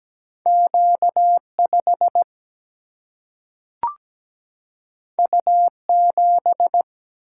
Morse code, heard as a keyed tone, about 17 wpm, 700 Hz. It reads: Q5 U7
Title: Single-Letter Single-Number 17wpm